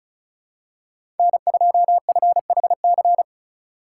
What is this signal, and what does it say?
Morse code, heard as a keyed tone, about 36 wpm, 700 Hz. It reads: N2FHC
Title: USA Call Signs - 4x Character Spacing - 35wpm